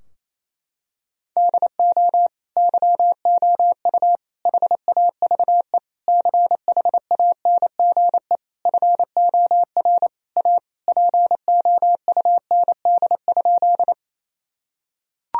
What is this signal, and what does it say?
Morse code, heard as a keyed tone, about 28 wpm, 700 Hz. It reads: DO YOU HAVE CHANGE FOR A POUND?
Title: Sentences from Top 1000 Words - Review Part 2 - 28wpm